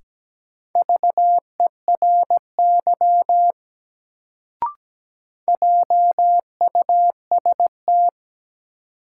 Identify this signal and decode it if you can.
Morse code, heard as a keyed tone, about 17 wpm, 700 Hz. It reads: VERY JUST
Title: Sets of 2 Words - New Word Drill - 17wpm